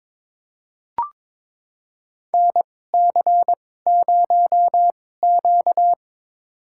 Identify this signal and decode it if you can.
Morse code, heard as a keyed tone, about 22 wpm, spaced as for 16 wpm, 700 Hz. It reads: NC0Q
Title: USA Call Signs - 4 Characters - 2x Character Spacing - 22wpm